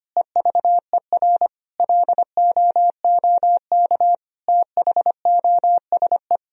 Morse code, heard as a keyed tone, about 25 wpm, 700 Hz. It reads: EVER LOOK THOSE
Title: Sentences from Top 1000 Words - Review Part 2 - 25wpm